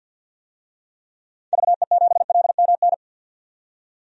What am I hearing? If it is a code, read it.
Morse code, heard as a keyed tone, about 52 wpm, 700 Hz. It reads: VE7LMN
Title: DXCC-100 Call Signs - 6 Characters - 50wpm